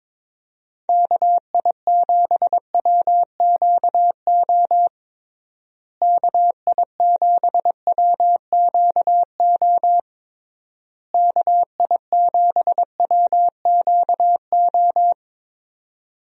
Morse code, heard as a keyed tone, about 22 wpm, 700 Hz. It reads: KI7WQO KI7WQO KI7WQO
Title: USA Call Signs - 6 Characters - Repeated 3x - 22wpm